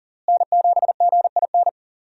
Morse code, heard as a keyed tone, about 40 wpm, 700 Hz. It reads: N7GIN
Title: USA Call Signs - 5 Characters - Repeated 3x - 40wpm